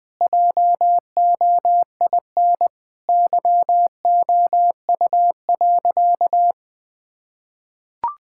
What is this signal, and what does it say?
Morse code, heard as a keyed tone, about 20 wpm, 700 Hz. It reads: JOIN YOU.
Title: Sentences with 5-Word Phrases - Spoken Beginning - 20wpm